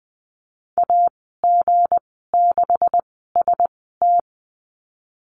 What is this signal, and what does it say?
Morse code, heard as a keyed tone, about 20 wpm, spaced as for 15 wpm, 700 Hz. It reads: AG6ST
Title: 173 Lesson 28g - Number 9 - Call Signs 20wpm